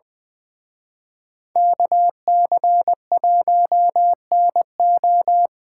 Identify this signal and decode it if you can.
Morse code, heard as a keyed tone, about 23 wpm, 700 Hz. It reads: KC1NO
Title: Speed Racing - USA 2x2 Call Signs 20wpm